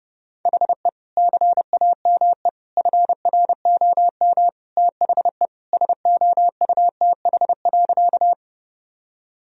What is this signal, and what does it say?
Morse code, heard as a keyed tone, about 30 wpm, 700 Hz. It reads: HE CAME FROM THE SOUTH.
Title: Sentences from Top 1000 Words - Review Part 2 - 30wpm